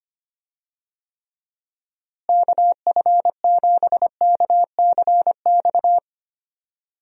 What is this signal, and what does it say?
Morse code, heard as a keyed tone, about 25 wpm, 700 Hz. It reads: KF7KCX
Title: USA Call Signs - 6 Characters - Repeated 3x - 25wpm